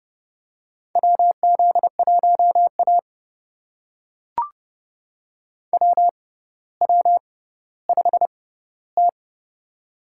Morse code, heard as a keyed tone, about 30 wpm, spaced as for 15 wpm, 700 Hz. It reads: WZ1A WW5T
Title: USA Call Signs - 6x Character Spacing - 30wpm